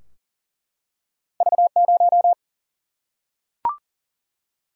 Morse code, heard as a keyed tone, about 39 wpm, 700 Hz. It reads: V0
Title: Single-Letter Single-Number 40wpm